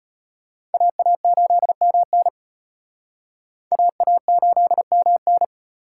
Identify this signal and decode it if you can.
Morse code, heard as a keyed tone, about 34 wpm, 700 Hz. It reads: AA8MN AA8MN